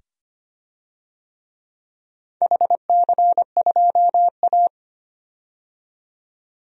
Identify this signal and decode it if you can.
Morse code, heard as a keyed tone, about 25 wpm, 700 Hz. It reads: HC2A
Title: DXCC-100 Call Signs - 4 Characters - 25wpm